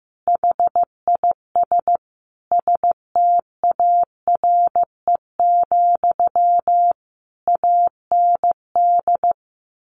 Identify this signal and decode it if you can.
Morse code, heard as a keyed tone, about 15 wpm, 700 Hz. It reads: HIS STARE, AND
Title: A Sound Of Thunder 15wpm